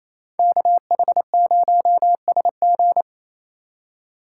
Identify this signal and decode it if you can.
Morse code, heard as a keyed tone, about 36 wpm, 700 Hz. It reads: KH0SG